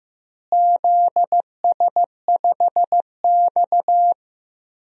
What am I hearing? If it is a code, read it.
Morse code, heard as a keyed tone, about 15 wpm, 700 Hz. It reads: ZS5X
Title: DXCC-100 Call Signs - 4 Characters - Repeated 3x - 15wpm